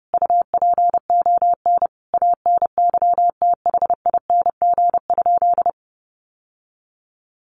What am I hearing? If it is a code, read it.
Morse code, heard as a keyed tone, about 30 wpm, 700 Hz. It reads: UPON ANYTHING?
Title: Sentences from Top 1200 Words 30wpm